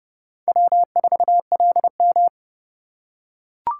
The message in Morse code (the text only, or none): W4LM